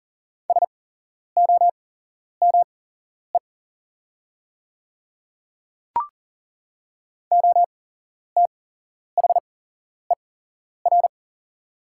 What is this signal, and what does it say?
Morse code, heard as a keyed tone, about 40 wpm, spaced as for 11 wpm, 700 Hz. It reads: SOME OTHER
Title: 8x Character Spacing - Top 200 Words - 40wpm